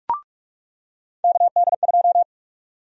Morse code, heard as a keyed tone, about 46 wpm, 700 Hz. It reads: KD2
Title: USA Call Sign Prefixes - 45wpm